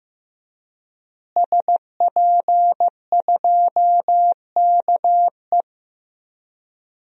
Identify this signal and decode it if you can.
Morse code, heard as a keyed tone, about 15 wpm, 700 Hz. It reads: SP2KE
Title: DXCC-100 Call Signs - 5 Characters - Repeated 3x - 15wpm